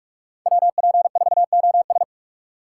Morse code, heard as a keyed tone, about 41 wpm, 700 Hz. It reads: WP4OS
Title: USA Call Signs - 5 Characters - 4x Character Spacing - 45wpm